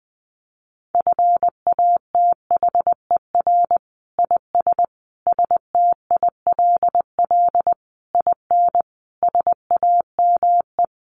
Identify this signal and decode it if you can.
Morse code, heard as a keyed tone, about 20 wpm, 700 Hz. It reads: FATHER IS STILL IN SAME
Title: Sentences from Top 1000 Words 20wpm